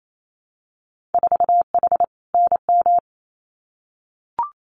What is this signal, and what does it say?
Morse code, heard as a keyed tone, about 28 wpm, 700 Hz. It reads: 4H NM
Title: Winter Field Day Exchanges Encore 28wpm